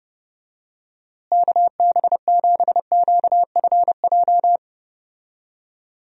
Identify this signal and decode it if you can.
Morse code, heard as a keyed tone, about 30 wpm, 700 Hz. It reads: KB7QFJ
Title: USA Call Signs - 6 Characters - Repeated 2x - 30wpm